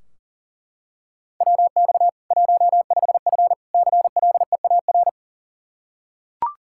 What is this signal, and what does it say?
Morse code, heard as a keyed tone, about 37 wpm, 700 Hz. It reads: WX 15F CLEAR
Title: Practice QSO - Exchange 2 - Info - 40wpm